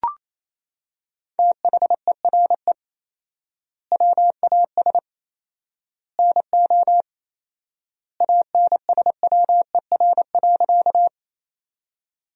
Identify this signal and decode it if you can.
Morse code, heard as a keyed tone, about 27 wpm, 700 Hz. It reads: THERE WAS NO ANSWER.
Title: 4x Word-Spacing - Sentences from Top 300 Words - 28wpm